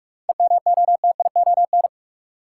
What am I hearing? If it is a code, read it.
Morse code, heard as a keyed tone, about 47 wpm, 700 Hz. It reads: EMOTION
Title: Sentences from Top 1600 Words - New Word Drill - 45wpm